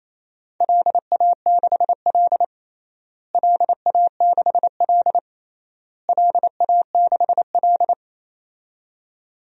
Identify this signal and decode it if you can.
Morse code, heard as a keyed tone, about 28 wpm, 700 Hz. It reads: LA6L LA6L LA6L
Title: DXCC-100 Call Signs - 4 Characters - Repeated 3x - 28wpm